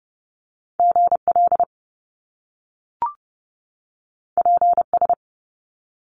Morse code, heard as a keyed tone, about 30 wpm, 700 Hz. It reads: GL PS
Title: Top Two Letter Combinations in Words 30wpm